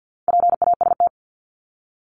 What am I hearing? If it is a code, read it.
Morse code, heard as a keyed tone, about 50 wpm, 700 Hz. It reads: PAST
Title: Top 501-750 Words 50wpm